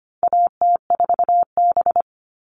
Morse code, heard as a keyed tone, about 25 wpm, 700 Hz. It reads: AT4B
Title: All Cut Numbers - 4-Digit 25wpm